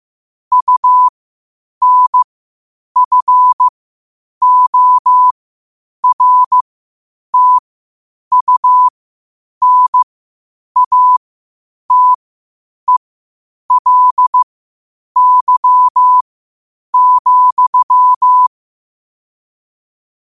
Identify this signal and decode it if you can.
Morse code, heard as a keyed tone, about 15 wpm, spaced as for 8 wpm, 1000 Hz. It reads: UNFORTUNATELY,
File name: qotdfiles / 8wpm